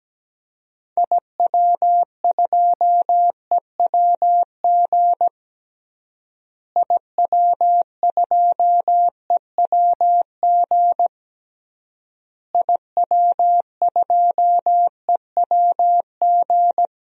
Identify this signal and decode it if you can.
Morse code, heard as a keyed tone, about 17 wpm, 700 Hz. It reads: IW2EWG IW2EWG IW2EWG
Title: DXCC-100 Call Signs - 6 Characters - Repeated 3x - 17wpm